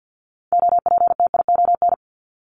Morse code, heard as a keyed tone, about 50 wpm, 700 Hz. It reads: OPTION
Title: Top 1001-1250 Words 50wpm